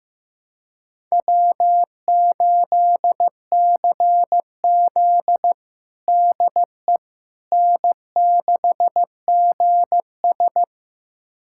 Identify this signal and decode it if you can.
Morse code, heard as a keyed tone, about 15 wpm, 700 Hz. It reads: W8CZ DE N6GS
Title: USA Callsign DE Callsign - 4 Characters - 15wpm